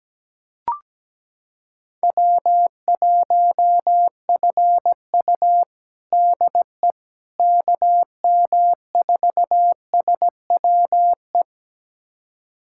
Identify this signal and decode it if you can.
Morse code, heard as a keyed tone, about 17 wpm, 700 Hz. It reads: W1FU DE KM4SWE
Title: USA Callsign DE Callsign 17wpm